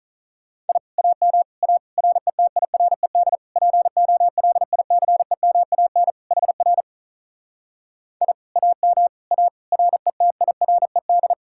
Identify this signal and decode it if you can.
Morse code, heard as a keyed tone, about 48 wpm, 700 Hz. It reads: I AM A RETIRED POLICEMAN HR I AM A RETIRED